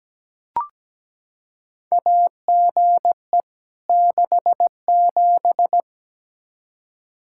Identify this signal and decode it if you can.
Morse code, heard as a keyed tone, about 17 wpm, 700 Hz. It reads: AGE 67
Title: Practice QSO - Exchange 3 - More Info - 17wpm